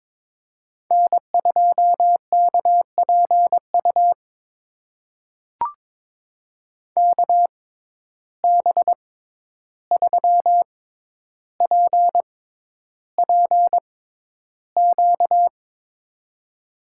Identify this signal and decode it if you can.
Morse code, heard as a keyed tone, about 22 wpm, spaced as for 11 wpm, 700 Hz. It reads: N2KPU KB3PPQ
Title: USA Call Signs - 6x Character Spacing - 22wpm